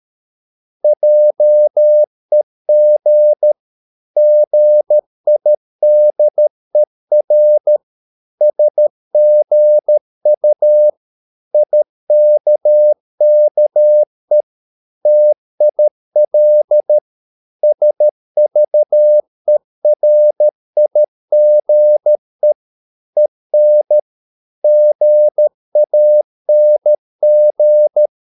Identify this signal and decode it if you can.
Morse code, heard as a keyed tone, about 13 wpm, 600 Hz. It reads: JEG GIDER SGU IKKE TIL SVERIGE EN GANG
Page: Ferie 13 wpm